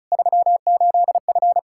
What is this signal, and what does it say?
Morse code, heard as a keyed tone, about 34 wpm, 700 Hz. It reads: 38F
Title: Practice QSO - Exchange 2 - Info - 35wpm